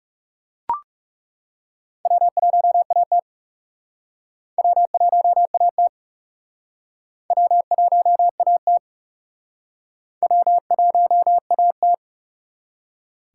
Morse code, heard as a keyed tone, about 34 wpm, 700 Hz. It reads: W1AT W1AT W1AT W1AT
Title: Speed Racing - USA 1x2 Call Signs 30wpm